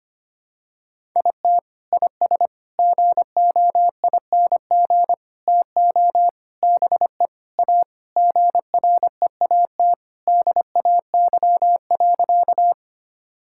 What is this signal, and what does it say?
Morse code, heard as a keyed tone, about 25 wpm, 700 Hz. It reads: IT IS GOING TO BE A GREAT DAY.
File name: practice / 25wpm / Sentences with 3-Word Phrases - 25wpm